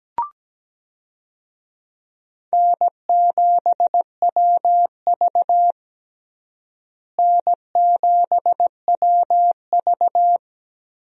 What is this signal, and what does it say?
Morse code, heard as a keyed tone, about 17 wpm, 700 Hz. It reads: N7WV N7WV
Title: USA Call Signs - 4 Characters - Repeated 2x - 17wpm